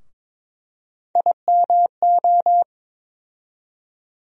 Morse code, heard as a keyed tone, about 21 wpm, 700 Hz. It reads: IMO
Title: USA Call Sign Postfixes - 3 Letters 22wpm